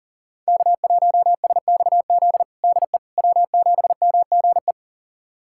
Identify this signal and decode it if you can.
Morse code, heard as a keyed tone, about 40 wpm, 700 Hz. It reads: K1SXZ DE W7MGE
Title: USA Callsign DE Callsign - 5 Characters - 40wpm